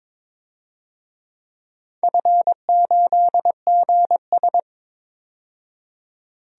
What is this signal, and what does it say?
Morse code, heard as a keyed tone, about 22 wpm, 700 Hz. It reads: F8GS
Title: DXCC-100 Call Signs - 4 Characters - 22wpm